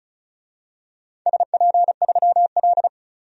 Morse code, heard as a keyed tone, about 36 wpm, 690 Hz. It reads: SP3L
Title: DXCC-100 Call Signs - 4 Characters - 35wpm